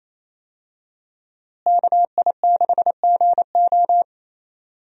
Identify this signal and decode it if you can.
Morse code, heard as a keyed tone, about 27 wpm, 700 Hz. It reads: KI6GO
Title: USA Call Signs - 5 Characters - Repeated 2x - 28wpm